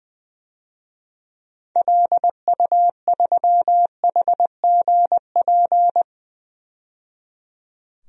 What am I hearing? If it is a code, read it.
Morse code, heard as a keyed tone, about 20 wpm, 700 Hz. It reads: LU3HGP
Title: DXCC-100 Call Signs - 6 Characters - 20wpm